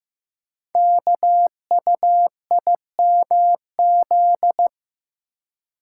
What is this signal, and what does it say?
Morse code, heard as a keyed tone, about 15 wpm, 700 Hz. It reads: KUIMZ
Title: Code Groups - 5 Letters - 15wpm